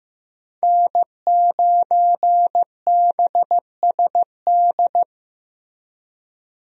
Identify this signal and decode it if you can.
Morse code, heard as a keyed tone, about 15 wpm, 700 Hz. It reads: N9BSD
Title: USA Call Signs - 5 Characters - Repeated 2x - 15wpm